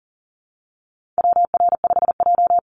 Morse code, heard as a keyed tone, about 38 wpm, 700 Hz. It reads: WR5J
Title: Call Signs - 40wpm